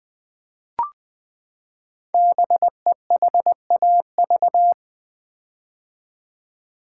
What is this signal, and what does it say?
Morse code, heard as a keyed tone, about 20 wpm, 700 Hz. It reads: BEHAV
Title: Build A Word - 5 Letters At A Time - 10-Letter Words - 20wpm